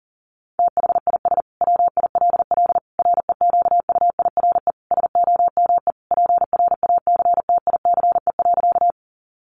Morse code, heard as a keyed tone, about 40 wpm, 700 Hz. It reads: THIS WILL REQUIRE SOME PRACTICE.
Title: Sentences from Top 900 Words 40wpm